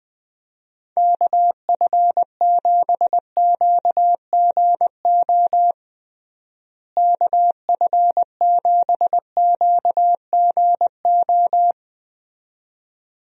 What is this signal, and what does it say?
Morse code, heard as a keyed tone, about 20 wpm, 700 Hz. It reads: KF7QGO KF7QGO
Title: USA Call Signs - 6 Characters - Repeated 2x - 20wpm